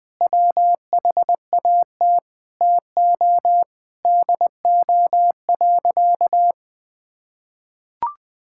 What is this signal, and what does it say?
Morse code, heard as a keyed tone, about 21 wpm, 700 Hz. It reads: WHAT TO DO.
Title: Sentences with 3-Word Phrases - 20wpm